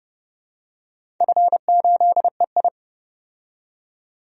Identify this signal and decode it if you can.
Morse code, heard as a keyed tone, about 31 wpm, 700 Hz. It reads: F8EI
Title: DXCC-100 Call Signs - 4 Characters - 30wpm